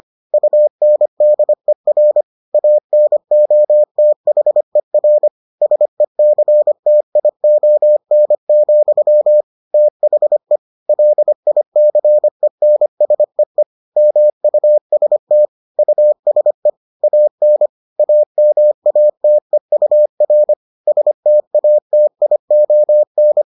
Morse code, heard as a keyed tone, about 25 wpm, 600 Hz. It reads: UNDER ANOTHER SECTION, THE LICENSEE MUST USE AN AMATEUR STATION